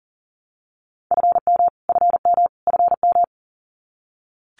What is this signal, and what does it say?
Morse code, heard as a keyed tone, about 37 wpm, 700 Hz. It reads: FM FM FM
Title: Abbreviations Repeated 3x - 40wpm